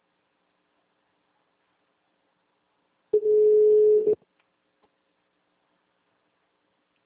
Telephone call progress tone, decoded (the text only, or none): ringback tone